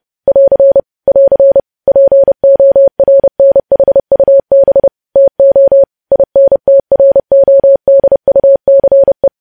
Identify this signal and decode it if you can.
Morse code, heard as a keyed tone, about 30 wpm, 550 Hz. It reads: <AR> <AR> PORNHUB TO INTRODUCE
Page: News Headlines in Morse Code at 30 WPM